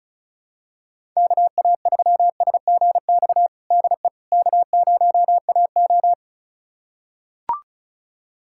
Morse code, heard as a keyed tone, about 34 wpm, 700 Hz. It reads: KA3SGX DE K0AO
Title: USA Callsign DE Callsign 35wpm